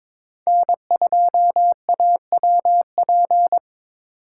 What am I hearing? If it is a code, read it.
Morse code, heard as a keyed tone, about 23 wpm, 700 Hz. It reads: N2AWP
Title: USA Call Signs - 5 Characters - 6x Character Spacing - 22wpm